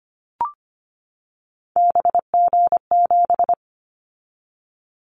Morse code, heard as a keyed tone, about 25 wpm, 700 Hz. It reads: BG7
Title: Callsign Prefixes for Top 100 DXCC - 25wpm